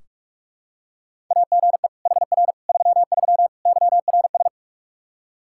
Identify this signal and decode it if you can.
Morse code, heard as a keyed tone, about 46 wpm, 700 Hz. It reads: AGE HR 33 YRS
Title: K1USN SST QSO Practice 45wpm